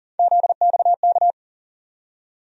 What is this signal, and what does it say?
Morse code, heard as a keyed tone, about 40 wpm, 700 Hz. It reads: ZXK
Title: USA Call Sign Postfixes - 3 Letters 40wpm